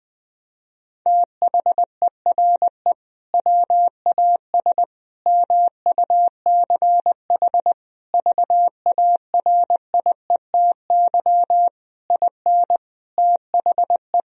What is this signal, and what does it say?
Morse code, heard as a keyed tone, about 21 wpm, 700 Hz. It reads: THERE WAS MUCH VARIETY IN THE
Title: Sentences from Top 1000 Words - Review Part 4 - 20wpm